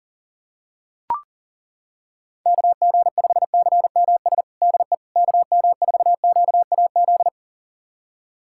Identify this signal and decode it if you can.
Morse code, heard as a keyed tone, about 40 wpm, 700 Hz. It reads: KG5CMS DE KM4QAZ
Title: USA Callsign DE Callsign - 6 Characters - 40wpm